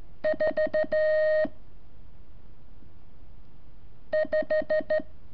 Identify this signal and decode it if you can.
Morse code, heard as a keyed tone, about 14 wpm, 620 Hz. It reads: 4 5
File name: curso / ejercicio 28